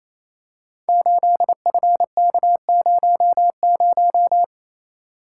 Morse code, heard as a keyed tone, about 28 wpm, 700 Hz. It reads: 8FK00
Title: Code Groups - 5 Letter:Numbers - 28wpm